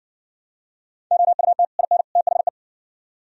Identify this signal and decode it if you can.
Morse code, heard as a keyed tone, about 54 wpm, 700 Hz. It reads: OUT IN THE